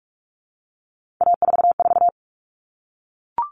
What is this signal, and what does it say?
Morse code, heard as a keyed tone, about 44 wpm, 700 Hz. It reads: A44
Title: All Cut Numbers - 3-Digit 45wpm